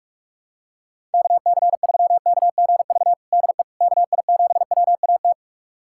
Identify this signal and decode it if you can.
Morse code, heard as a keyed tone, about 46 wpm, 700 Hz. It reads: KC3KGV DE KI7WAT
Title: USA Callsign DE Callsign 45wpm